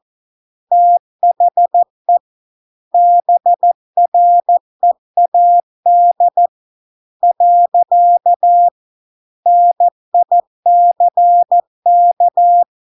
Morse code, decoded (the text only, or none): THE BREAD . NICK